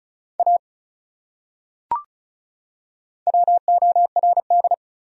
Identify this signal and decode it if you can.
Morse code, heard as a keyed tone, about 36 wpm, 700 Hz. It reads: A WORD
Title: Top 100 Words 35wpm